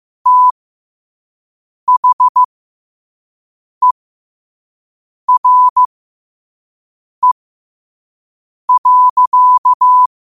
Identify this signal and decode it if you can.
Morse code, heard as a keyed tone, about 15 wpm, spaced as for 5 wpm, 1000 Hz. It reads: THERE.
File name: qotdfiles / 5wpm